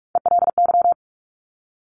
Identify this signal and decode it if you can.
Morse code, heard as a keyed tone, about 46 wpm, 700 Hz. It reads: ELY
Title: Top Three Letter Combinations in Words 45wpm